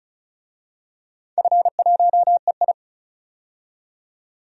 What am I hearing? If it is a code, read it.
Morse code, heard as a keyed tone, about 36 wpm, 700 Hz. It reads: F1EI
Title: DXCC-100 Call Signs - 4 Characters - 35wpm